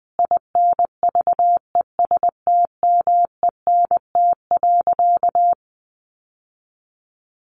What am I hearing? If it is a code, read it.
Morse code, heard as a keyed tone, about 20 wpm, 700 Hz. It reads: INVESTMENT.
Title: Sentences from Top 700 Words 20wpm